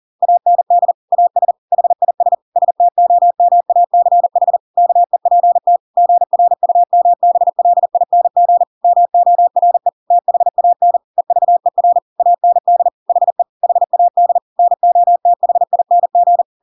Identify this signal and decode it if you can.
Morse code, heard as a keyed tone, about 40 wpm, 700 Hz. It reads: AND AS HIS STOMACH KEPT GRUMBLING MORE THAN EVER AND HE HAD NOTHING
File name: Pinocchio / Pinocchio ch5 40 40 ik1hge